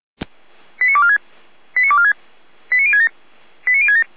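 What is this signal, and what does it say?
Five-tone selective call (ZVEI), heard as quick runs of five tones, tones of about 70 ms. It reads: 89246 89246 89067 89067
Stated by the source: ZVEI five-tone selective call received on an SDR.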